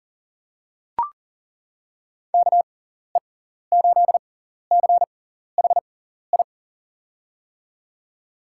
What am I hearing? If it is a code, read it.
Morse code, heard as a keyed tone, about 40 wpm, spaced as for 14 wpm, 700 Hz. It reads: KE8CHI
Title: USA Call Signs - 6 Characters - 6x Character Spacing - 40wpm